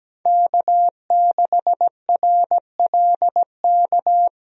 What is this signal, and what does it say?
Morse code, heard as a keyed tone, about 17 wpm, 700 Hz. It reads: K6RLK
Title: USA Call Signs - 5 Characters - 17wpm